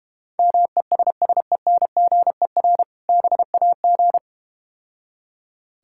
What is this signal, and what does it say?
Morse code, heard as a keyed tone, about 32 wpm, 700 Hz. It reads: MESSENGER BAG